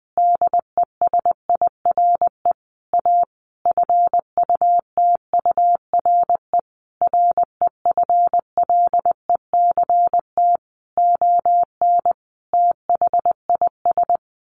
Morse code, heard as a keyed tone, about 20 wpm, 700 Hz. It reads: DESIRE A FUTURE REFLECT ON THIS
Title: Sentences from Top 1500 Words 20wpm